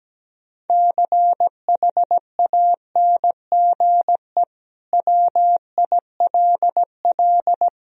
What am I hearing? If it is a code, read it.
Morse code, heard as a keyed tone, about 17 wpm, 700 Hz. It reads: CHANGE WILL
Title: Sentences from Top 1000 Words - Review Part 2 - 17wpm